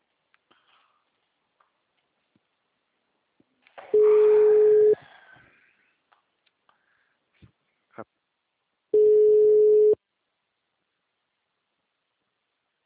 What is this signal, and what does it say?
Ringback tone (Europe, 425 Hz).
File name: out- -2007- - - .30093